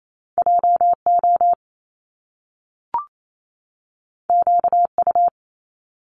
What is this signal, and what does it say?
Morse code, heard as a keyed tone, about 28 wpm, 700 Hz. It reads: JO QU
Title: Top Two Letter Combinations in Words 28wpm